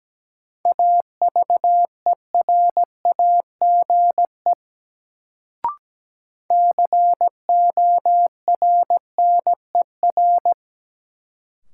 Morse code, heard as a keyed tone, about 17 wpm, 700 Hz. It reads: AVERAGE CORNER
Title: Sets of 5 Words - New Word Drill - 17wpm